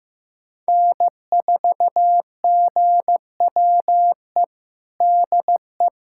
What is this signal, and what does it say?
Morse code, heard as a keyed tone, about 15 wpm, 700 Hz. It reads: N4GWE DE
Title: USA Callsign DE Callsign 15wpm